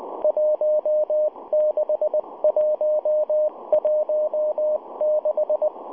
Morse code, heard as a keyed tone, about 20 wpm, 610 Hz, with radio noise. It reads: 16116